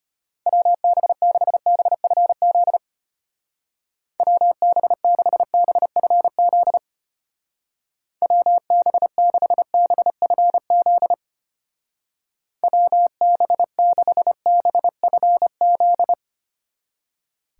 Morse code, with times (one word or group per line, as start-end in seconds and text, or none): 0.5-2.8 WB6BFZ
4.2-6.8 WB6BFZ
8.2-11.1 WB6BFZ
12.6-16.1 WB6BFZ